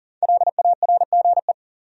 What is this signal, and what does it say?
Morse code, heard as a keyed tone, about 54 wpm, spaced as for 47 wpm, 700 Hz. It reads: LARGE
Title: Speed-Racing Top 200 Words 40wpm